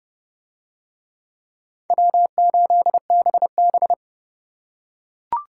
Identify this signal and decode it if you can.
Morse code, heard as a keyed tone, about 29 wpm, 700 Hz. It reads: W8BB